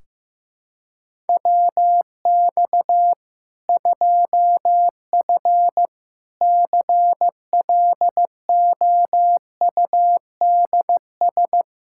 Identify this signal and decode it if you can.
Morse code, heard as a keyed tone, about 15 wpm, 700 Hz. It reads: WX 2F CLOUDS
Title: Practice QSO - Exchange 2 - Info - 15wpm